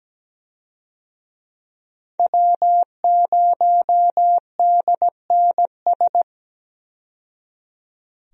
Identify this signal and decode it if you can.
Morse code, heard as a keyed tone, about 17 wpm, 700 Hz. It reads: W0DNS
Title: USA Call Signs - 5 Characters - 17wpm